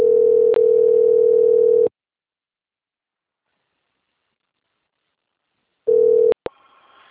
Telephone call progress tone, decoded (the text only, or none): ringback tone